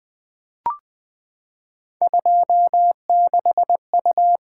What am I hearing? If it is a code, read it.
Morse code, heard as a keyed tone, about 20 wpm, 700 Hz. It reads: 26U
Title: Three Letter-Number Combinations - 20wpm